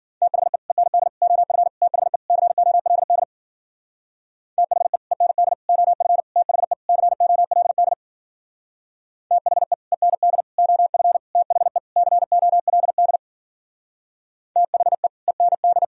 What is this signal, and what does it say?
Morse code, heard as a keyed tone, about 52 wpm, 700 Hz. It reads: THE END OF THE COLD THE END OF THE COLD THE END OF THE COLD THE END